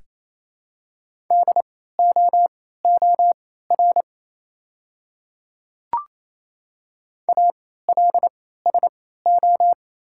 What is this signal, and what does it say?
Morse code, heard as a keyed tone, about 28 wpm, spaced as for 16 wpm, 700 Hz. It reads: DOOR ALSO
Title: 3x Character Spacing - Top 300 Words - 28wpm